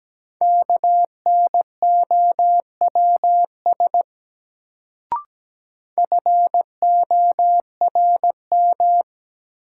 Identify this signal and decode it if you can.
Morse code, heard as a keyed tone, about 17 wpm, 700 Hz. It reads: KNOWS FORM
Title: Sentences from Top 1000 Words - Review Part 1 - New Word Drill - 17wpm